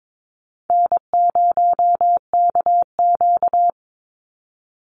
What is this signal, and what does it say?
Morse code, heard as a keyed tone, about 22 wpm, 700 Hz. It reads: N0KQ
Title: Easy Call Signs - 22wpm